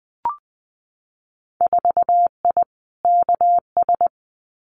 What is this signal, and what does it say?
Morse code, heard as a keyed tone, about 20 wpm, 700 Hz. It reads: 4I KS
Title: Winter Field Day Exchanges Encore 20wpm